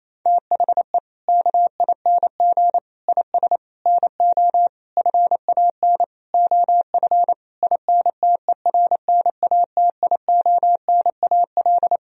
Morse code, heard as a keyed tone, about 28 wpm, 700 Hz. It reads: THE KING IS NO FAN OF INTERNATIONAL
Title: Sentences from Top 1000 Words - Review Part 4 - 28wpm